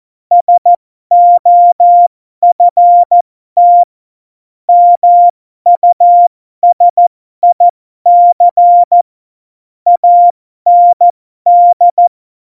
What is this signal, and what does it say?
Morse code, heard as a keyed tone, about 14 wpm, spaced as for 12 wpm, 700 Hz. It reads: SOFT MUSIC AND
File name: Morsekurs / Witze